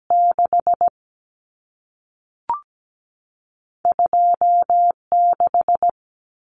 Morse code, heard as a keyed tone, about 17 wpm, 700 Hz. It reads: 6 26
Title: Common Cut Numbers - 2-Digit 17wpm